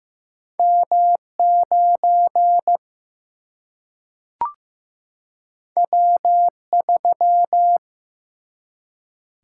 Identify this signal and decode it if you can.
Morse code, heard as a keyed tone, about 16 wpm, 700 Hz. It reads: M9 W3
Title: Single-Letter Single-Number 15wpm